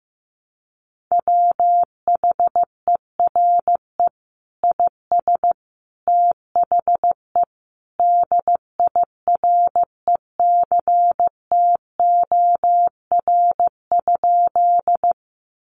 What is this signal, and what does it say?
Morse code, heard as a keyed tone, about 15 wpm, 700 Hz. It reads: WHERE IS THE DIRECTOR?
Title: Sentences from Top 1300 Words 15wpm